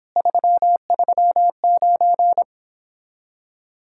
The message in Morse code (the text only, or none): 339